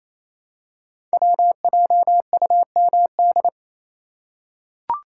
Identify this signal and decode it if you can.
Morse code, heard as a keyed tone, about 28 wpm, 700 Hz. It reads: WJUMD
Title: Code Groups - 5 Letters - 28wpm